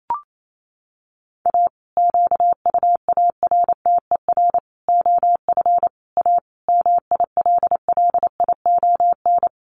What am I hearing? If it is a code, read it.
Morse code, heard as a keyed tone, about 28 wpm, 700 Hz. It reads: A QUARTER OF A MILLION
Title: Sets of 5 Words Encore2 28wpm